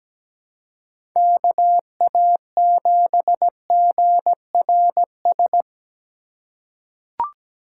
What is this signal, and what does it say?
Morse code, heard as a keyed tone, about 17 wpm, 700 Hz. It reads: KA7GRS
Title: USA Call Signs - 6 Characters - 3x Character Spacing - 17wpm